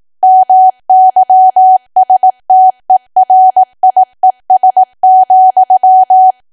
Morse code, heard as a keyed tone, about 18 wpm, 750 Hz. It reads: MYSTERIES,
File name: Archive / 18 WPM / 18WPM